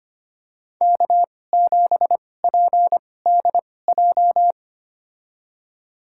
Morse code, heard as a keyed tone, about 25 wpm, spaced as for 18 wpm, 700 Hz. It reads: K7PDJ
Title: USA Call Signs - 5 Characters - 2x Character Spacing - 25wpm